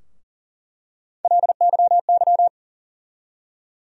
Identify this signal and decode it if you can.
Morse code, heard as a keyed tone, about 40 wpm, 700 Hz. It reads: LYY
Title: USA Call Sign Postfixes - 3 Letters 40wpm